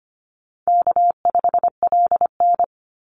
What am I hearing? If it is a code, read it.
Morse code, heard as a keyed tone, about 25 wpm, 700 Hz. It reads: K5LN
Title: Easy Call Signs Repeated 3x - 25wpm